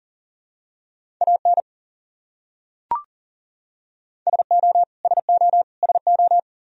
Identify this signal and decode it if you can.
Morse code, heard as a keyed tone, about 40 wpm, 700 Hz. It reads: AN SO SO SO
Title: Top 100 Words Repeated 40wpm